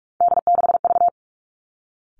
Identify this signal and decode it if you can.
Morse code, heard as a keyed tone, about 46 wpm, 700 Hz. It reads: D6V
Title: Three Challenging Letter-Number Combinations 45wpm